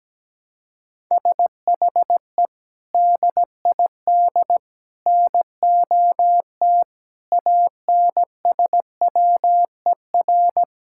Morse code, heard as a keyed tone, about 17 wpm, 700 Hz. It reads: SHE DID NOT ANSWER
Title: Sentences from Top 1000 Words - Review Part 5 - 17wpm